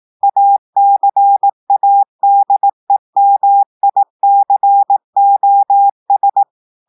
Morse code, heard as a keyed tone, about 18 wpm, 800 Hz. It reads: ACADEMICOS
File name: audios / portugais / portugais texte en clair 1 vitesse 18 mots minutes